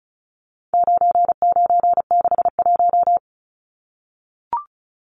Morse code, heard as a keyed tone, about 34 wpm, 700 Hz. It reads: 9961
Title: Numbers - 4-Digit 35wpm